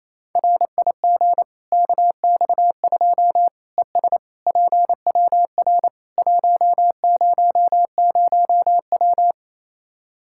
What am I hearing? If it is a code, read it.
Morse code, heard as a keyed tone, about 28 wpm, 700 Hz. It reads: RIG KX2 ES PWR 100W
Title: Practice QSO - Exchange 2 - Info - 28wpm